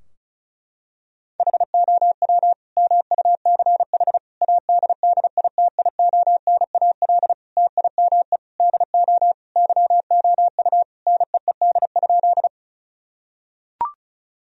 Morse code, heard as a keyed tone, about 36 wpm, 700 Hz. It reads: HOW MUCH ADDITIONAL TIME DO YOU NEED?
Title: Sentences from Top 1000 Words - Review Part 4 - 35wpm